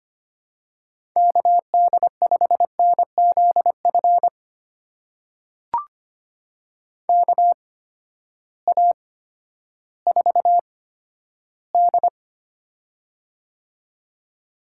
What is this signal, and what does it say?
Morse code, heard as a keyed tone, about 25 wpm, spaced as for 12 wpm, 700 Hz. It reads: KD5NZF KA4D
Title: USA Call Signs - 8x Character Spacing - 25wpm